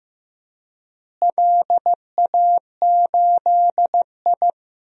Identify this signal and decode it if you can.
Morse code, heard as a keyed tone, about 16 wpm, 700 Hz. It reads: LA8I
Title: DXCC-100 Call Signs - 4 Characters - Repeated 3x - 15wpm